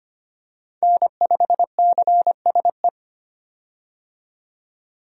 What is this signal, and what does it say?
Morse code, heard as a keyed tone, about 25 wpm, 700 Hz. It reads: N5CSE
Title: USA Call Signs - 5 Characters - 25wpm